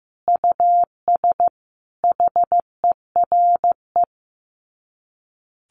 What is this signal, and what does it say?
Morse code, heard as a keyed tone, about 15 wpm, 700 Hz. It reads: US HERE
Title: Sets of 5 Words 15wpm